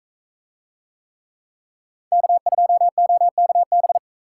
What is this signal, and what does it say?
Morse code, heard as a keyed tone, about 42 wpm, 700 Hz. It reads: K2OKB